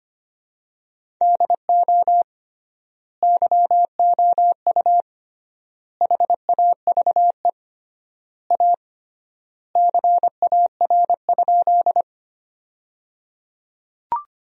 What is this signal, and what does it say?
Morse code, heard as a keyed tone, about 25 wpm, 700 Hz. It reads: DO YOU HAVE A CAR?
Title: 3x Word-Spacing - Sentences from Top 300 Words - 25wpm